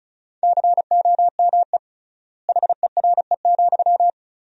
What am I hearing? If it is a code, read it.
Morse code, heard as a keyed tone, about 36 wpm, 700 Hz. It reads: COME HERE,
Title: 3x Word-Spacing - Sentences from Top 300 Words - 35wpm